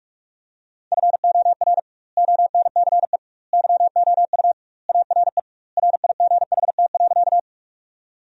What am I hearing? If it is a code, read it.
Morse code, heard as a keyed tone, about 43 wpm, 700 Hz. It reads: FOR ONCE YOU ARE RIGHT.
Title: 2x Word-Spacing - Sentences from Top 300 Words - 45wpm